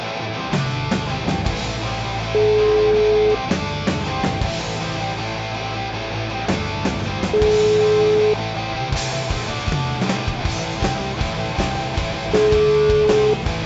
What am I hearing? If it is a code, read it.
Ringback tone (Europe, 425 Hz), with noise.